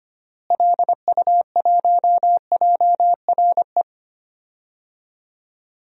Morse code, heard as a keyed tone, about 25 wpm, 700 Hz. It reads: LU1JRE
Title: DXCC-100 Call Signs - 6 Characters - Repeated 3x - 25wpm